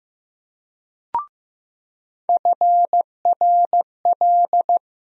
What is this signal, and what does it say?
Morse code, heard as a keyed tone, about 15 wpm, 700 Hz. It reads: FRL
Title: USA Call Sign Postfixes - 3 Letters 15wpm